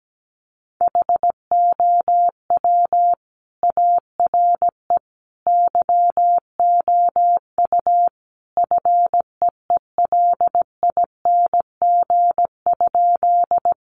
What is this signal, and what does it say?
Morse code, heard as a keyed tone, about 17 wpm, 700 Hz. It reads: HOW ARE YOU FEELING?
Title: Sentences from Top 1200 Words 17wpm